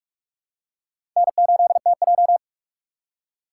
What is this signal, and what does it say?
Morse code, heard as a keyed tone, about 40 wpm, 700 Hz. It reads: N8TJ